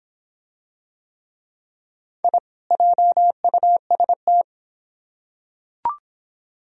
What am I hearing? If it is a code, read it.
Morse code, heard as a keyed tone, about 24 wpm, 700 Hz. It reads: I JUST